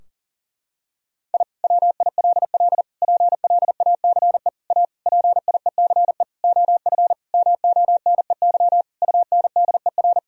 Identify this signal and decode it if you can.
Morse code, heard as a keyed tone, about 40 wpm, 700 Hz. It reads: I WILL PLACE A PIECE OF MONEY UNDER
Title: Sentences from Top 1000 Words - Review Part 2 - 40wpm